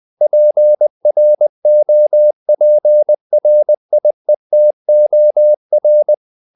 Morse code, heard as a keyed tone, about 20 wpm, 600 Hz. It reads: PROPRIETOR
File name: the old man and the sea part 2 20wpm